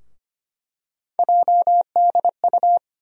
Morse code, heard as a keyed tone, about 25 wpm, 700 Hz. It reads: JDU
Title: USA Call Sign Postfixes - 3 Letters 25wpm